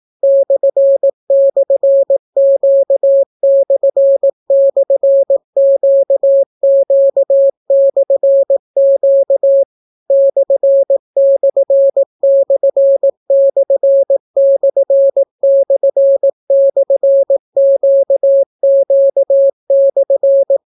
Morse code, heard as a keyed tone, about 18 wpm, 550 Hz. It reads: //Q//QQ/Q ///////QQ/